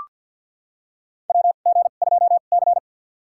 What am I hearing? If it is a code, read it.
Morse code, heard as a keyed tone, about 45 wpm, spaced as for 34 wpm, 700 Hz. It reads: WG2C
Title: USA Call Signs - 4 Characters - 2x Character Spacing - 50wpm